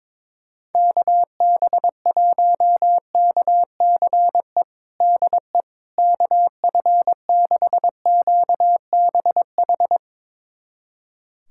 Morse code, heard as a keyed tone, about 22 wpm, 700 Hz. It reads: KB1KCE DE KF6QBH
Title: USA Callsign DE Callsign 22wpm